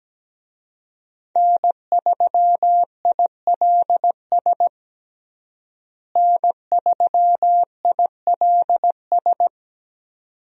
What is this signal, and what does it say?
Morse code, heard as a keyed tone, about 17 wpm, 700 Hz. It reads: N3ILS N3ILS
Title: USA Call Signs - 5 Characters - Repeated 2x - 17wpm